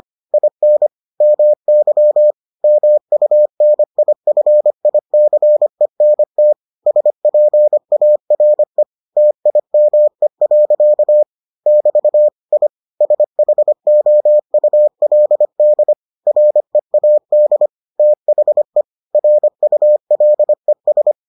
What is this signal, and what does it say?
Morse code, heard as a keyed tone, about 25 wpm, 600 Hz. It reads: IN MY MUNIFICENT SPARE TIME. <BT> I SHOULD READ THE RULES